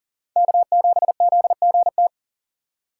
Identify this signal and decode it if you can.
Morse code, heard as a keyed tone, about 40 wpm, 700 Hz. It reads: K7ZGT
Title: USA Call Signs - 5 Characters - 6x Character Spacing - 40wpm